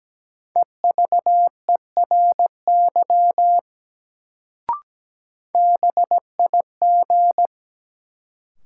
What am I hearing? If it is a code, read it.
Morse code, heard as a keyed tone, about 17 wpm, 700 Hz. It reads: EVERY BIG
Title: Sentences from Top 200 Words - New Word Drill - 17wpm